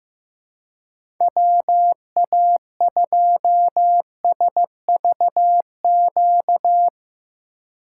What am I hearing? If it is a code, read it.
Morse code, heard as a keyed tone, about 17 wpm, 700 Hz. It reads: WA2SVQ
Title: Speed Racing - USA 2x3 Call Signs 15wpm